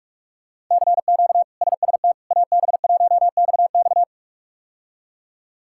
Morse code, heard as a keyed tone, about 46 wpm, 700 Hz. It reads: CQ SST AB1XX
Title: K1USN SST QSO Practice - 45wpm